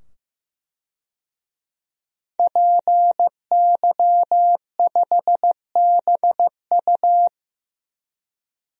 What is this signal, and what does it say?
Morse code, heard as a keyed tone, about 15 wpm, 700 Hz. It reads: PY5BU
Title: DXCC-100 Call Signs - 5 Characters - 15wpm